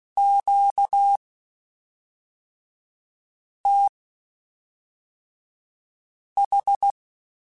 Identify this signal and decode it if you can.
Morse code, heard as a keyed tone, about 16 wpm, spaced as for 3 wpm, 800 Hz. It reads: QTH
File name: (spaced) / Q-Calls Spaced-mp3s / Q Calls sp Once @5